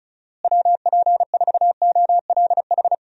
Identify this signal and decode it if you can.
Morse code, heard as a keyed tone, about 34 wpm, 700 Hz. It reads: WP4OLH
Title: USA Call Signs - 6 Characters - Repeated 3x - 35wpm